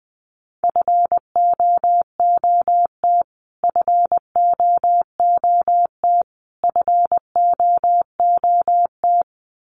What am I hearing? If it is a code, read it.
Morse code, heard as a keyed tone, about 20 wpm, 700 Hz. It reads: FOOT FOOT FOOT
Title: Top 500 Words Repeated 20wpm